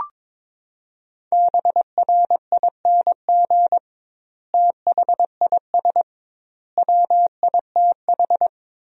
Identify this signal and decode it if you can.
Morse code, heard as a keyed tone, about 22 wpm, 700 Hz. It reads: BRING THIS WITH
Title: 2x Word-Spacing - Sentences from Top 500 Words - 22wpm